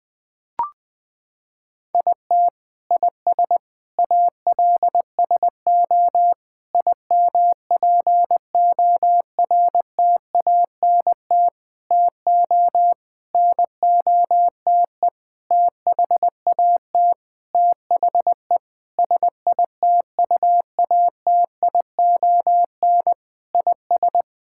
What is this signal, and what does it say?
Morse code, heard as a keyed tone, about 20 wpm, 700 Hz. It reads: IT IS ALSO IMPORTANT TO NOTE THAT THE SITUATION IS
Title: Sentences with 8-Word Phrases - 20wpm